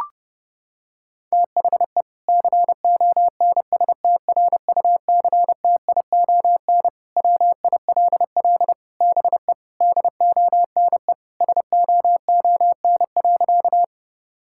Morse code, heard as a keyed tone, about 32 wpm, 700 Hz. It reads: THE CONSTRUCTION WILL BE DONE SOON.
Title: Sentences from Top 1600 Words 30wpm